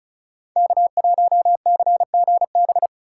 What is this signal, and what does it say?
Morse code, heard as a keyed tone, about 36 wpm, 690 Hz. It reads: K1CGB
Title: Code Groups - 5 Letter:Numbers - 35wpm